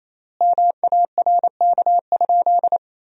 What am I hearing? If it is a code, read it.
Morse code, heard as a keyed tone, about 28 wpm, 700 Hz. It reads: MARK?
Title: 3x Word-Spacing - Sentences from Top 300 Words - 28wpm